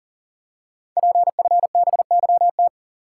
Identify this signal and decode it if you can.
Morse code, heard as a keyed tone, about 38 wpm, 690 Hz. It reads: PFBYT
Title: Code Groups - 5 Letters - 40wpm